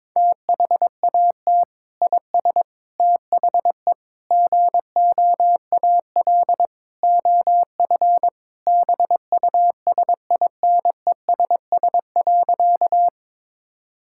Morse code, heard as a keyed tone, about 21 wpm, 700 Hz. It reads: THAT IS THE GOAL OF BUSINESS.
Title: Sentences from Top 1000 Words - Review Part 5 - 22wpm